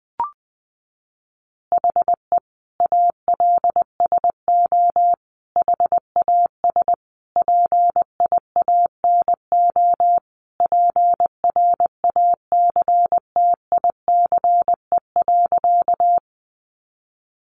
Morse code, heard as a keyed tone, about 20 wpm, 700 Hz. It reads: HE ALSO HAS PIANO PRACTICE.
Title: Additional Easy Phrases 20wpm